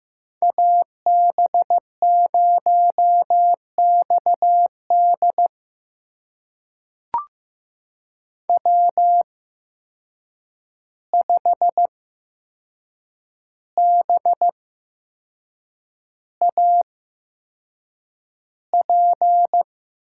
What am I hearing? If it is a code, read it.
Morse code, heard as a keyed tone, about 15 wpm, spaced as for 6 wpm, 700 Hz. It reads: AB0XD W5BAP
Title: USA Call Signs - 5 Characters - 8x Character Spacing - 15wpm